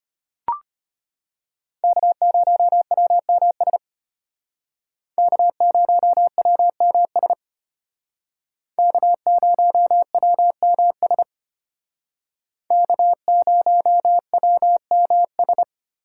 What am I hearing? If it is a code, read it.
Morse code, heard as a keyed tone, about 30 wpm, 700 Hz. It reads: K0WMS K0WMS K0WMS K0WMS
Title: Speed Racing - USA 1x3 Call Signs 25wpm